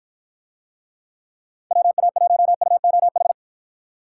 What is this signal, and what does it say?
Morse code, heard as a keyed tone, about 53 wpm, 700 Hz. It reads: WA1UOH